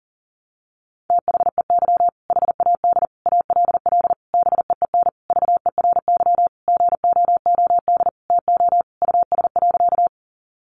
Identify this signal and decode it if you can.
Morse code, heard as a keyed tone, about 40 wpm, 700 Hz. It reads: THEY HAD ALL BEEN VERY GOOD TO US.
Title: Sentences from Top 200 Words 40wpm